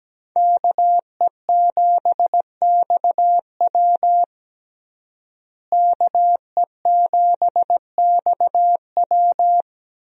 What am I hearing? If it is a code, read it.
Morse code, heard as a keyed tone, about 17 wpm, 700 Hz. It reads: KE7XW KE7XW
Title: USA Call Signs - 5 Characters - Repeated 2x - 17wpm